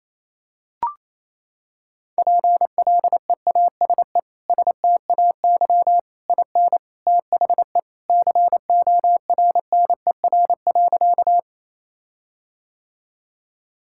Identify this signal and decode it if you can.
Morse code, heard as a keyed tone, about 27 wpm, 700 Hz. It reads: PLEASE STAY IN THE CORNER.
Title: Sentences from Top 1600 Words 28wpm